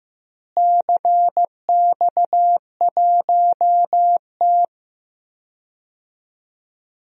Morse code, heard as a keyed tone, about 15 wpm, 700 Hz. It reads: CX1T
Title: DXCC-100 Call Signs - 4 Characters - Repeated 2x - 15wpm